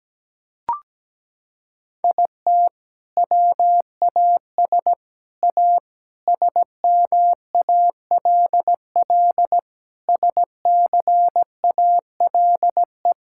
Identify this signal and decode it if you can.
Morse code, heard as a keyed tone, about 17 wpm, 700 Hz. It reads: IT WAS A SMALL SCALE
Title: Sentences from Top 1000 Words - Review Part 5 - 17wpm